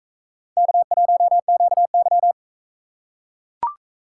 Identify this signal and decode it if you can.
Morse code, heard as a keyed tone, about 41 wpm, 700 Hz. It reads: K1QY